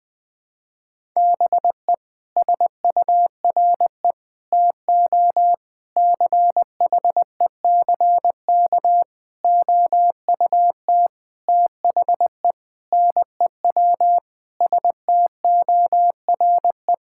Morse code, heard as a keyed tone, about 21 wpm, 700 Hz. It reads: BE SURE TO CHECK OUT THE NEW STORE
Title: Sentences with 6-Word Phrases - 20wpm